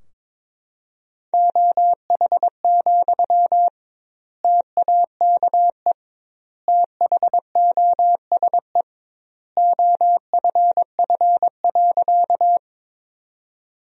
Morse code, heard as a keyed tone, about 22 wpm, 700 Hz. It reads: OH, TAKE THOSE OFF.
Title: 2x Word-Spacing - Sentences from Top 500 Words - 22wpm